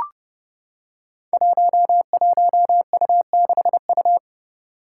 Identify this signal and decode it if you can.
Morse code, heard as a keyed tone, about 30 wpm, 700 Hz. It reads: 11U6U
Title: Five Challenging Letter-Number Combinations 30wpm